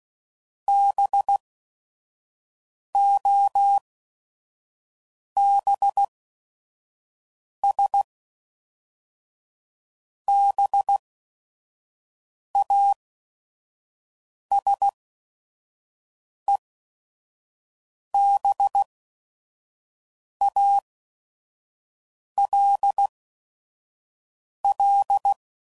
Morse code, heard as a keyed tone, about 16 wpm, spaced as for 5 wpm, 800 Hz. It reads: BOBS BASEBALL